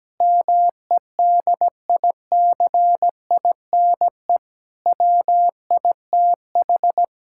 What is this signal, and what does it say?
Morse code, heard as a keyed tone, about 17 wpm, 700 Hz. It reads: MEDICINE WITH
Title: Sentences from Top 1000 Words - Review Part 6 - 17wpm